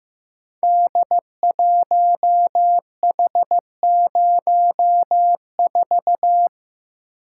Morse code, heard as a keyed tone, about 15 wpm, 700 Hz. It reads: D1H04
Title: Code Groups - 5 Letter:Numbers - 15wpm